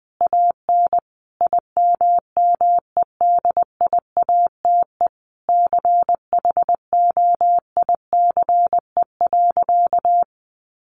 Morse code, heard as a keyed tone, about 20 wpm, 700 Hz. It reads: AN IMMEDIATE CHOICE.
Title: Sentences from Top 1200 Words 20wpm